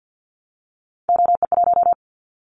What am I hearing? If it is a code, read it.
Morse code, heard as a keyed tone, about 50 wpm, 700 Hz. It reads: OE1
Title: Callsign Prefixes for Top 25 DXCC - Repeated 3x - 50wpm